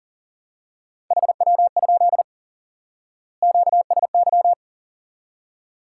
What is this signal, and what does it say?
Morse code, heard as a keyed tone, about 40 wpm, 700 Hz. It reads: HW? QSY
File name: IQE Course 3 - Lesson 89 - YAGI - Two Elements - 40wpm